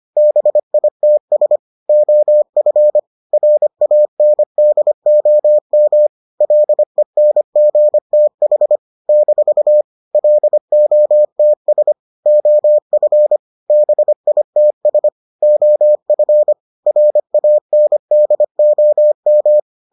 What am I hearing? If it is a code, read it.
Morse code, heard as a keyed tone, about 25 wpm, 600 Hz. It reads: BITS OF RANDOM LENGTH - LOTS OF BITS OF RANDOM